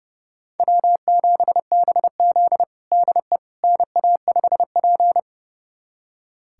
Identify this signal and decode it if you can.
Morse code, heard as a keyed tone, about 30 wpm, 700 Hz. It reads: W7BZ DE NA5P
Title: USA Callsign DE Callsign - 4 Characters - 30wpm